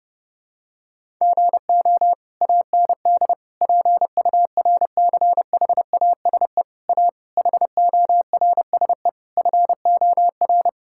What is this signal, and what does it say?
Morse code, heard as a keyed tone, about 30 wpm, 700 Hz. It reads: GO AND PURCHASE A HORSE FOR
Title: Sentences from Top 1000 Words - Review Part 5 - 30wpm